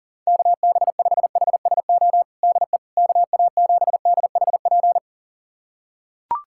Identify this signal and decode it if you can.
Morse code, heard as a keyed tone, about 40 wpm, 700 Hz. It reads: KB5HSO DE KA7DHP
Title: USA Callsign DE Callsign 40wpm